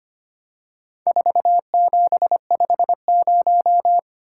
Morse code, heard as a keyed tone, about 25 wpm, 700 Hz. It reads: 4750
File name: practice / 25wpm / Four Letter-Number Combinations - 25wpm